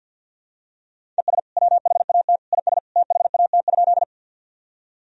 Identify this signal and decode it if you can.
Morse code, heard as a keyed tone, about 53 wpm, 700 Hz. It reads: ES WHAT IS THAT?
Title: Sentences from Top 100 Words and Abbreviations 50wpm